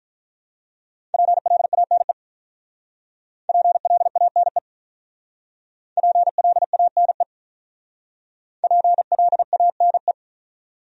Morse code, heard as a keyed tone, about 48 wpm, 700 Hz. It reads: PLANE PLANE PLANE PLANE